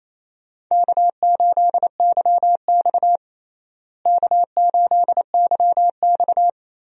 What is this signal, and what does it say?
Morse code, heard as a keyed tone, about 28 wpm, 700 Hz. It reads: K8YX K8YX
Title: USA Call Signs - 4 Characters - Repeated 2x - 28wpm